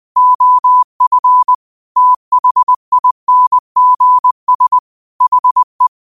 Morse code, heard as a keyed tone, about 20 wpm, 1000 Hz. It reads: OF THINGS HE